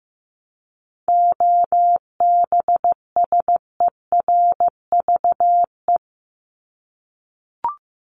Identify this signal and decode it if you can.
Morse code, heard as a keyed tone, about 15 wpm, 700 Hz. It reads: OBSERVE
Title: ED Suffix - Top 100 Words with Base Words 15wpm